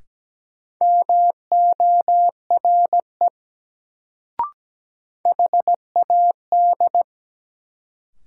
Morse code, heard as a keyed tone, about 17 wpm, 700 Hz. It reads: MORE HAD
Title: Sentences from Top 100 Words - New Word Drill - 17wpm